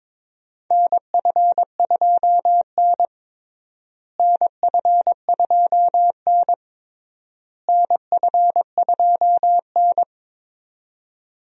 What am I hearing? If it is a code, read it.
Morse code, heard as a keyed tone, about 22 wpm, 700 Hz. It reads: NF2N NF2N NF2N
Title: USA Call Signs - 4 Characters - Repeated 3x - 22wpm